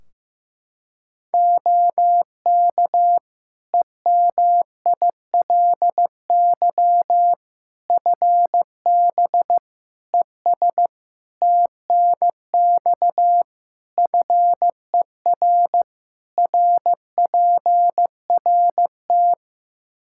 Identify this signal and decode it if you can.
Morse code, heard as a keyed tone, about 15 wpm, 700 Hz. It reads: OK EMILY FB ES TNX FER RPRT
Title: Practice QSO - Exchange 2 - Info - 15wpm